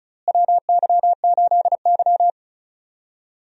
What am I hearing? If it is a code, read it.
Morse code, heard as a keyed tone, about 34 wpm, 700 Hz. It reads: WY8Y
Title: USA Call Signs - 4 Characters - Repeated 2x - 35wpm